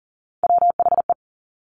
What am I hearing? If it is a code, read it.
Morse code, heard as a keyed tone, about 42 wpm, 700 Hz. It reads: WHE
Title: Top Three Letter Combinations in Words 40wpm